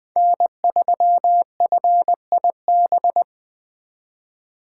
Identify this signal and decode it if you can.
Morse code, heard as a keyed tone, about 23 wpm, 700 Hz. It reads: N3FIB
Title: Speed Racing - USA 1x3 Call Signs 20wpm